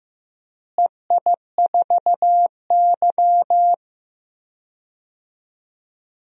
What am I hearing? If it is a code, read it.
Morse code, heard as a keyed tone, about 15 wpm, 700 Hz. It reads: EI4Y
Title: DXCC-100 Call Signs - 4 Characters - Repeated 2x - 15wpm